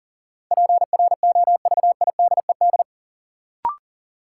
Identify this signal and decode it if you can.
Morse code, heard as a keyed tone, about 40 wpm, 700 Hz. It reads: PROVIDED
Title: Sentences from Top 1500 Words - New Word Drill Alt - 40wpm